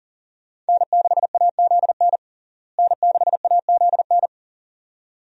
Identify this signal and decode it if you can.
Morse code, heard as a keyed tone, about 40 wpm, 700 Hz. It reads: N6AZN N6AZN
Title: USA Call Signs - 5 Characters - Repeated 2x - 40wpm